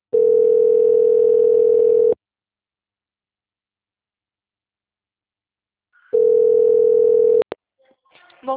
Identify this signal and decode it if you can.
Ringback tone (North America), 440 + 480 Hz.